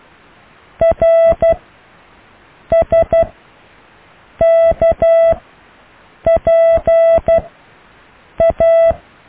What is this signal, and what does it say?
Morse code, heard as a keyed tone, about 12 wpm, spaced as for 6 wpm, 660 Hz. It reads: RSKPA
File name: cw / Alle letters